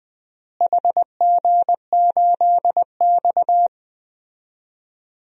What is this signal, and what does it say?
Morse code, heard as a keyed tone, about 21 wpm, 700 Hz. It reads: HG8X
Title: DXCC-100 Call Signs - 4 Characters - Repeated 3x - 20wpm